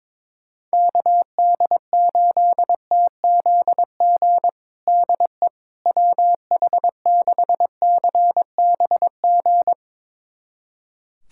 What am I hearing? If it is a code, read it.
Morse code, heard as a keyed tone, about 22 wpm, 700 Hz. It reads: KD8TZG DE WH6CBG
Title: USA Callsign DE Callsign - 6 Characters - 22wpm